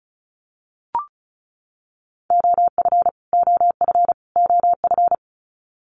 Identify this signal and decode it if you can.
Morse code, heard as a keyed tone, about 34 wpm, 700 Hz. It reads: OF OF OF
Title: Top 500 Words Repeated 35wpm